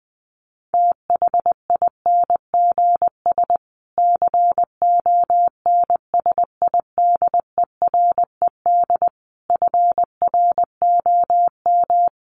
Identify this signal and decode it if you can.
Morse code, heard as a keyed tone, about 20 wpm, 700 Hz. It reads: THINGS CONSIDERED FROM
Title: Sets of 5 Words Encore 20wpm